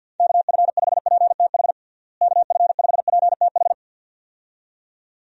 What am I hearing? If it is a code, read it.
Morse code, heard as a keyed tone, about 47 wpm, 700 Hz. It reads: KF5PTH KF5PTH
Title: USA Call Signs - 6 Characters - Repeated 2x - 50wpm